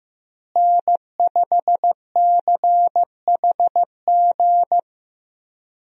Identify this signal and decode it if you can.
Morse code, heard as a keyed tone, about 15 wpm, 700 Hz. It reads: N5CHG
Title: USA Call Signs - 5 Characters - Repeated 2x - 15wpm